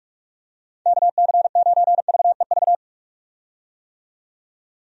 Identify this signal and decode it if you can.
Morse code, heard as a keyed tone, about 46 wpm, 700 Hz. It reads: KC9VEV
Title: USA Call Signs - 6 Characters - 45wpm